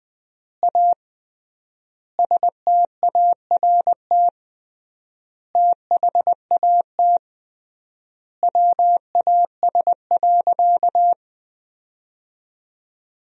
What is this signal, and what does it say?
Morse code, heard as a keyed tone, about 20 wpm, 700 Hz. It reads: A START THAT WAS.
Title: 3x Word-Spacing - Sentences from Top 300 Words - 20wpm